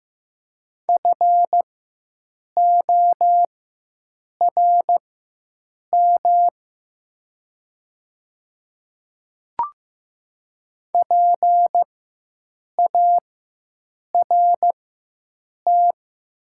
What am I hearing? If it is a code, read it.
Morse code, heard as a keyed tone, about 15 wpm, spaced as for 7 wpm, 700 Hz. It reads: FORM PART
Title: 4x Character Spacing - Top 300 Words - 15wpm